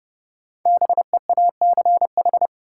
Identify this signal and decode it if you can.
Morse code, heard as a keyed tone, about 30 wpm, 700 Hz. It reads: BEACH
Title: Sentences from Top 1500 Words - New Word Drill Alt - 30wpm